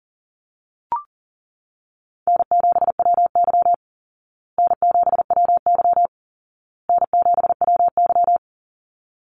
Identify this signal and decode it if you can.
Morse code, heard as a keyed tone, about 40 wpm, 700 Hz. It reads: N7WY N7WY N7WY
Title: Easy Call Signs Repeated 3x - 40wpm